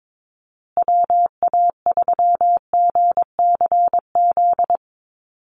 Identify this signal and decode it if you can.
Morse code, heard as a keyed tone, about 22 wpm, 690 Hz. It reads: WA3GCZ
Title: USA Call Signs Repeated 3x 22wpm